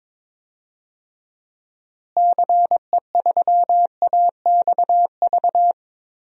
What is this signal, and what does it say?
Morse code, heard as a keyed tone, about 22 wpm, 700 Hz. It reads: CE3AXV
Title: DXCC-100 Call Signs - 6 Characters - Repeated 2x - 22wpm